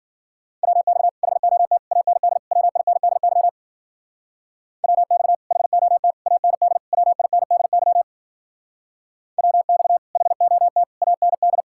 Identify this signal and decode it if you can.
Morse code, heard as a keyed tone, about 53 wpm, 700 Hz. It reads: WX HOT AND WINDY WX HOT AND WINDY WX HOT AND